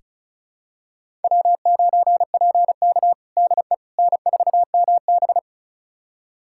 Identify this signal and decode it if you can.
Morse code, heard as a keyed tone, about 36 wpm, 700 Hz. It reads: W9PK DE N4MB
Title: USA Callsign DE Callsign - 4 Characters - 35wpm